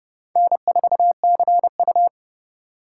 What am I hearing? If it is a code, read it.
Morse code, heard as a keyed tone, about 30 wpm, 700 Hz. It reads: N4CU
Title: USA Call Signs - 4 Characters - 2x Character Spacing - 30wpm